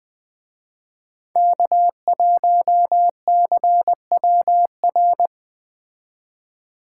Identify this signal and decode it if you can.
Morse code, heard as a keyed tone, about 21 wpm, 700 Hz. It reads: K1CWR